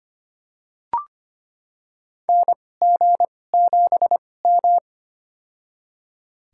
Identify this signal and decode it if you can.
Morse code, heard as a keyed tone, about 25 wpm, spaced as for 18 wpm, 700 Hz. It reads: NG7M
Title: USA Call Signs - 2x Character Spacing - 25wpm